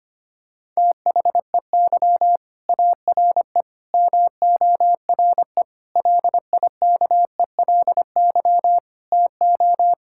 Morse code, heard as a keyed tone, about 25 wpm, 700 Hz. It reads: THEY ARE MORE LIKELY TO
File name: practice / 25wpm / Sentences with 4-Word Phrases - 25wpm